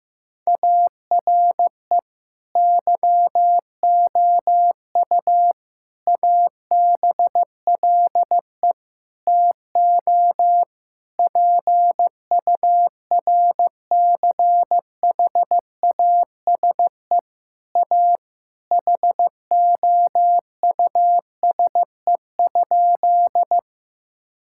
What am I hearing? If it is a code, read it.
Morse code, heard as a keyed tone, about 15 wpm, 700 Hz. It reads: ARE YOU ABLE TO PURCHASE A HOUSE?
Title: Sentences from Top 1000 Words - Review Part 5 - 15wpm